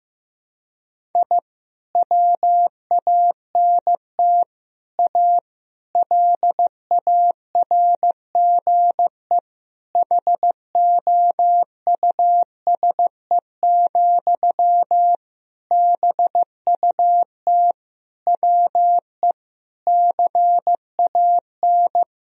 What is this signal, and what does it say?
Morse code, heard as a keyed tone, about 15 wpm, 700 Hz. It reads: I WANT A LARGE HOUSE, BUT WE CAN
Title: Sentences from Top 1000 Words - Review Part 1 - 15wpm